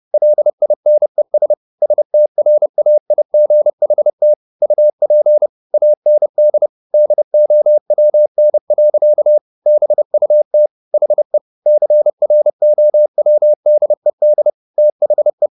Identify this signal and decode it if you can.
Morse code, heard as a keyed tone, about 30 wpm, 600 Hz. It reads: LINES STRAIGHT UP AND DOWN. BUT HE CROWDED THE
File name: the old man and the sea / the old man and the sea 1 30wpm